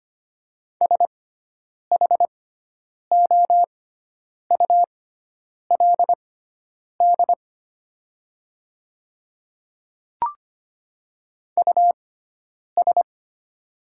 Morse code, heard as a keyed tone, about 25 wpm, spaced as for 9 wpm, 700 Hz. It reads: SHOULD US
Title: 6x Character Spacing - Top 300 Words - 25wpm